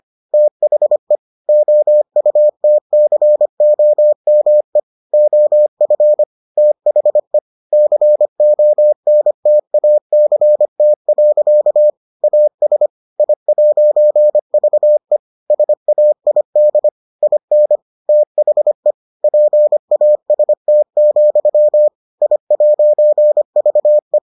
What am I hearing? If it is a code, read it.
Morse code, heard as a keyed tone, about 25 wpm, 600 Hz. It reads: THE OUTCOME OF THE CONTACT. AS I'VE SAID IN THE PAST, I'VE